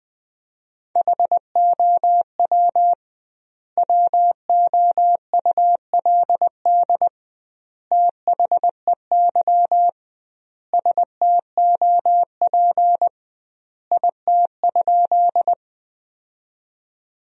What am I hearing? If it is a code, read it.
Morse code, heard as a keyed tone, about 20 wpm, 700 Hz. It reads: HOW WOULD THEY STOP IT?
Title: 2x Word-Spacing - Sentences from Top 300 Words - 20wpm